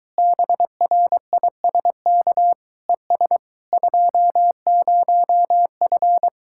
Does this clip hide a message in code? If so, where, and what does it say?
Yes: Morse code from 0.2 to 6.3 s, heard as a keyed tone, about 22 wpm, 700 Hz. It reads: BRISK ES 20F